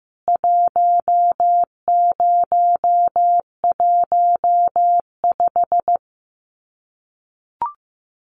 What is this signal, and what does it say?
Morse code, heard as a keyed tone, about 15 wpm, 700 Hz. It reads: 1015
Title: CWT Exchange 15wpm